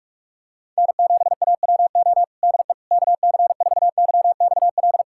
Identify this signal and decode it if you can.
Morse code, heard as a keyed tone, about 44 wpm, 690 Hz. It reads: N7AWO DE KC4YXL
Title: USA Callsign DE Callsign 45wpm